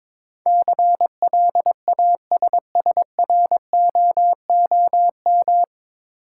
Morse code, heard as a keyed tone, about 22 wpm, 700 Hz. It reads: CLASSROOM
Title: Sentences from Top 1500 Words - New Word Drill Alt - 22wpm